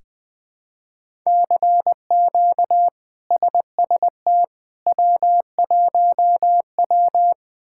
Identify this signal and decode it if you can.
Morse code, heard as a keyed tone, about 20 wpm, 700 Hz. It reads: CQ SST W1W
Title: K1USN SST QSO Practice - 20wpm